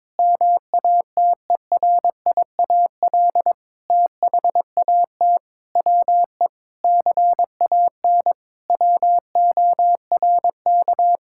Morse code, heard as a keyed tone, about 22 wpm, 700 Hz. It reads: MATERIAL THAT WE CAN WORK
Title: Sentences from Top 1000 Words - Review Part 5 - 22wpm